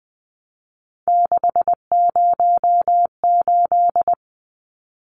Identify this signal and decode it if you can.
Morse code, heard as a keyed tone, about 20 wpm, 700 Hz. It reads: 608
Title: Numbers - 3-Digit 20wpm